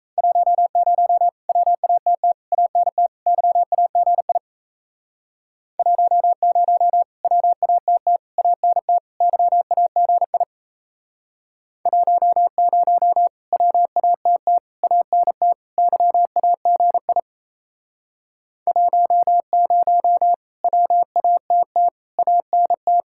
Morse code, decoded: 10 WATT ANT YAGI 10 WATT ANT YAGI 10 WATT ANT YAGI 10 WATT ANT

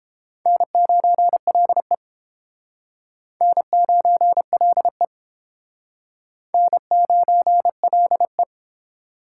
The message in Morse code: N9LE N9LE N9LE

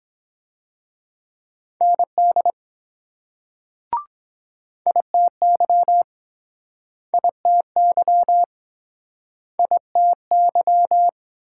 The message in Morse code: ND ITY ITY ITY

